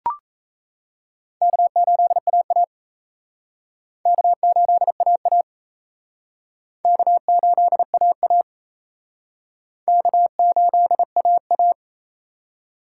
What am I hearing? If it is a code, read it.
Morse code, heard as a keyed tone, about 34 wpm, 700 Hz. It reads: K8AA K8AA K8AA K8AA
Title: Speed Racing - USA 1x2 Call Signs 28wpm